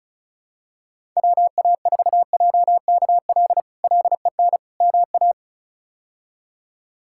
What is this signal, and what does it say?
Morse code, heard as a keyed tone, about 34 wpm, 700 Hz. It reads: WA4JKL LEN MA
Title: K1USN SST QSO Practice - 35wpm